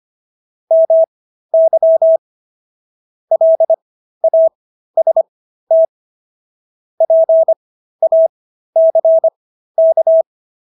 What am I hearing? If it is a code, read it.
Morse code, heard as a keyed tone, about 25 wpm, spaced as for 13 wpm, 650 Hz. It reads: MY LAST PACK
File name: SS106 13